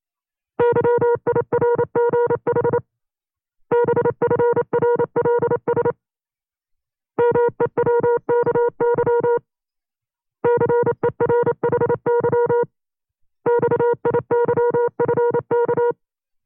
Morse code, heard as a keyed tone, about 29 wpm, 480 Hz. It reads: YIRGH BFRLS MEWKY CERHY XIYFK
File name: CW-practice/25WPM